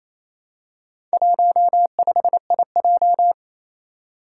Morse code, heard as a keyed tone, about 28 wpm, 700 Hz. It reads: 15IJ
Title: Four Challenging Letter-Number Combinations 28wpm